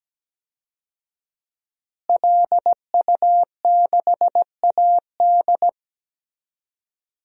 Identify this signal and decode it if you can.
Morse code, heard as a keyed tone, about 17 wpm, 700 Hz. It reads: LU6AD
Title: DXCC-100 Call Signs - 5 Characters - Repeated 3x - 17wpm